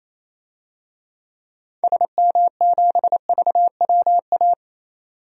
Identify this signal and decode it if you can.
Morse code, heard as a keyed tone, about 28 wpm, 700 Hz. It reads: SM7VWA
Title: DXCC-100 Call Signs - 6 Characters - 28wpm